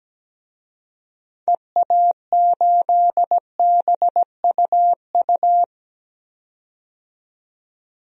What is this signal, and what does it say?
Morse code, heard as a keyed tone, about 17 wpm, 700 Hz. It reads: EA8BUU
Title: DXCC-100 Call Signs - 6 Characters - 17wpm